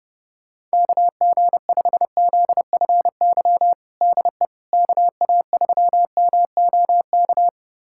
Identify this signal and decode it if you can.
Morse code, heard as a keyed tone, about 32 wpm, 700 Hz. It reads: KG5ZFY DE KA3MOK
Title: USA Callsign DE Callsign - 6 Characters - 30wpm